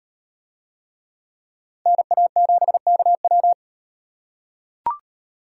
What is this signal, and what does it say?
Morse code, heard as a keyed tone, about 35 wpm, 700 Hz. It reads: NA7KW